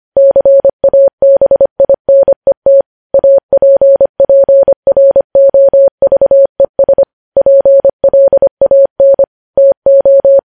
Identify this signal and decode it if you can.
Morse code, heard as a keyed tone, about 25 wpm, 550 Hz. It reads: CABINET APPROVES PLAN TO
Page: news-headlines-in-morse-code-at-25-wpm